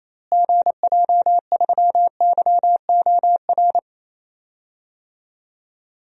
Morse code, heard as a keyed tone, about 28 wpm, 700 Hz. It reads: GJ3YOR
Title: DXCC-100 Call Signs - 6 Characters - Repeated 3x - 28wpm